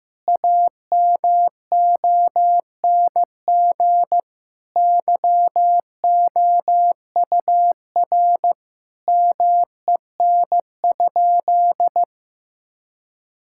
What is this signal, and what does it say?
Morse code, heard as a keyed tone, about 15 wpm, 700 Hz. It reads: AMONG YOUR MEN?
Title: Sentences from Top 1000 Words - Review Part 6 - 15wpm